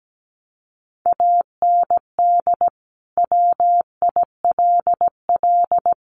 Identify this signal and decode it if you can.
Morse code, heard as a keyed tone, about 17 wpm, 700 Hz. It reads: AND WILL
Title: Sentences from Top 1500 Words 17wpm